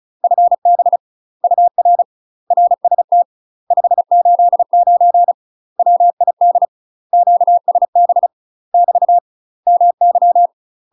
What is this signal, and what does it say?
Morse code, heard as a keyed tone, about 34 wpm, 700 Hz. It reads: FB UR RST 589 WID QSB <BT> MY
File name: UK MASTER QSOs TRAINING / 35WPM / UK MASTER QSOs 0021